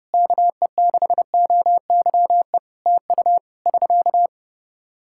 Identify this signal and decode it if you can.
Morse code, heard as a keyed tone, about 30 wpm, 700 Hz. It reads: KE6OYE TU <SK>
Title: Practice QSO - Exchange 4 - Ending - 30wpm